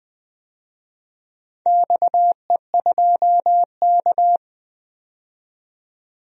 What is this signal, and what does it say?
Morse code, heard as a keyed tone, about 20 wpm, 700 Hz. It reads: XE2K
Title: DXCC-100 Call Signs - 4 Characters - 20wpm